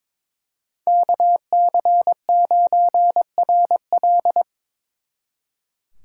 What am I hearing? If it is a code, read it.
Morse code, heard as a keyed tone, about 26 wpm, 700 Hz. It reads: KC9RL
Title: Speed Racing - USA 2x2 Call Signs 22wpm